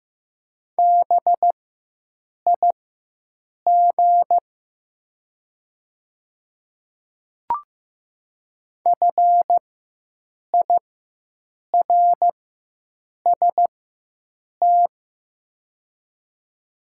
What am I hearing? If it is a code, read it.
Morse code, heard as a keyed tone, about 15 wpm, spaced as for 7 wpm, 700 Hz. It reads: BIG FIRST
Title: 4x Character Spacing - Top 200 Words - 15wpm